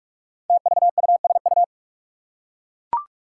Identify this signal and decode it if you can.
Morse code, heard as a keyed tone, about 44 wpm, 700 Hz. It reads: TVUSU
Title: Five Challenging Letter-Number Combinations 45wpm